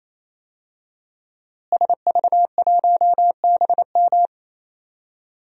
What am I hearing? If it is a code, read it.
Morse code, heard as a keyed tone, about 28 wpm, 700 Hz. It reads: SV1BM
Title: DXCC-100 Call Signs - 5 Characters - Repeated 3x - 28wpm